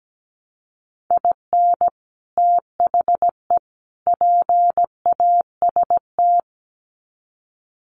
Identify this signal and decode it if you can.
Morse code, heard as a keyed tone, about 17 wpm, 700 Hz. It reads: IN THE PAST
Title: Sets of 3 Words 17wpm